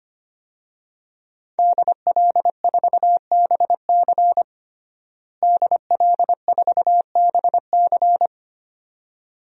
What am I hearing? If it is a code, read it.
Morse code, heard as a keyed tone, about 25 wpm, 700 Hz. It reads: DL4BC DL4BC
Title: DXCC-100 Call Signs - 5 Characters - Repeated 2x - 25wpm